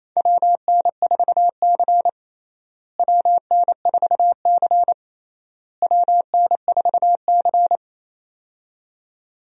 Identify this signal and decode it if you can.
Morse code, heard as a keyed tone, about 28 wpm, 700 Hz. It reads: WN4C WN4C WN4C
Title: USA Call Signs - 4 Characters - Repeated 3x - 28wpm